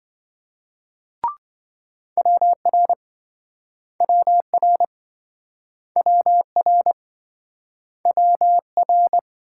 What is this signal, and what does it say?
Morse code, heard as a keyed tone, about 28 wpm, 700 Hz. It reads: WR WR WR WR